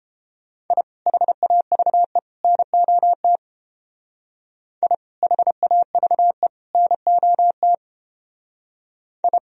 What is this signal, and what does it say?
Morse code, heard as a keyed tone, about 31 wpm, 700 Hz. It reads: I HAVE NOT I HAVE NOT I